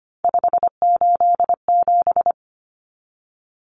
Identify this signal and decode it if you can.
Morse code, heard as a keyed tone, about 25 wpm, 700 Hz. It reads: 587
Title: Numbers - 3-Digit 25wpm